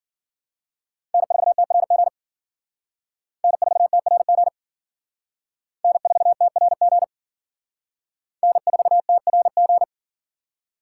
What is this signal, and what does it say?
Morse code, heard as a keyed tone, about 50 wpm, 700 Hz. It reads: N4TRG N4TRG N4TRG N4TRG
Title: Speed Racing - USA 1x3 Call Signs 40wpm